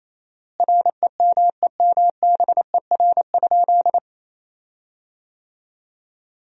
Morse code, heard as a keyed tone, about 28 wpm, 700 Hz. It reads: REMEMBER?
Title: 4x Word-Spacing - Sentences from Top 400 Words - 28wpm